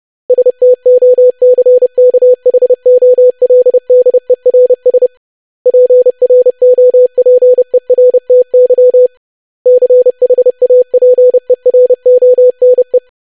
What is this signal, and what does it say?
Morse code, heard as a keyed tone, about 30 wpm, 500 Hz. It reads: STOCKHOLDERS PROPERTY CHAPERONE
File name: lesson / web / longwords-25wpm